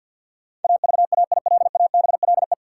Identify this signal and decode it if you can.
Morse code, heard as a keyed tone, about 53 wpm, 700 Hz. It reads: AVAILABLE
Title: Sentences from Top 600 Words - New Word Drill - 50wpm